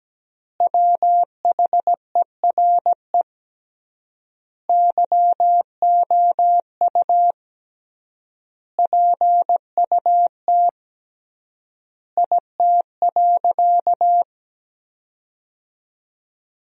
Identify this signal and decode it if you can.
Morse code, heard as a keyed tone, about 17 wpm, 700 Hz. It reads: WHERE YOU PUT IT.
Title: 3x Word-Spacing - Sentences from Top 200 Words - 17wpm